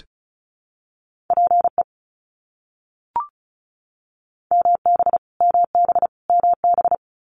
Morse code, heard as a keyed tone, about 34 wpm, 700 Hz. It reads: PE MB MB MB
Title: Canadian Provinces and Territories - Repeated 3x - 35wpm